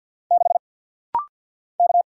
Morse code, heard as a keyed tone, about 50 wpm, 700 Hz. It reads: 6 K
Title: Single Letter-Number Alt - Rapid-Fire 50wpm